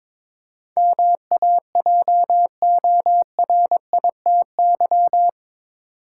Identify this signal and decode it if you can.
Morse code, heard as a keyed tone, about 22 wpm, 700 Hz. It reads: MAJORITY
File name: practice / 22wpm / Eight Letter Words - 22wpm